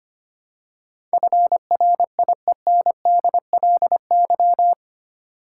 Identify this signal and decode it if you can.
Morse code, heard as a keyed tone, about 25 wpm, 700 Hz. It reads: FRIENDLY
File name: 8 Letter Words - First 3 Letters Given - 25wpm